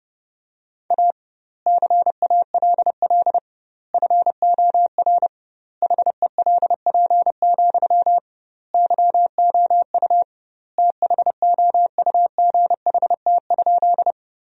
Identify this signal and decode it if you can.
Morse code, heard as a keyed tone, about 30 wpm, 700 Hz. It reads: A CALL FOR HELP, YOU THOUGHT?
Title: 2x Word-Spacing - Sentences from Top 300 Words - 30wpm